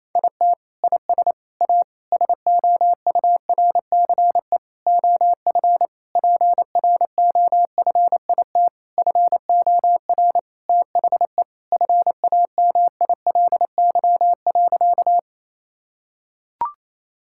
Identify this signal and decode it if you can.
Morse code, heard as a keyed tone, about 28 wpm, 700 Hz. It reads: IT IS A SOURCE OF PROFIT FOR THE FAMILY.
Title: Sentences from Top 1000 Words - Review Part 5 - 28wpm